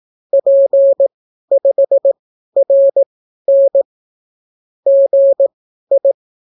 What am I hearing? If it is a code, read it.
Morse code, heard as a keyed tone, about 18 wpm, spaced as for 12 wpm, 550 Hz. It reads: P5RN GI